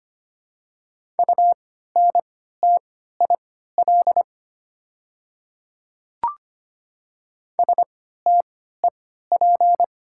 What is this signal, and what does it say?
Morse code, heard as a keyed tone, about 25 wpm, spaced as for 14 wpm, 690 Hz. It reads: UNTIL STEP
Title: 3x Character Spacing - Top 500 Words - 25wpm